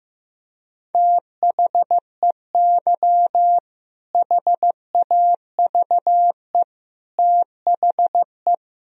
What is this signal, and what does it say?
Morse code, heard as a keyed tone, about 15 wpm, 700 Hz. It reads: THEY HAVE THE
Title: Sentences from Top 1000 Words - Review Part 3 - 15wpm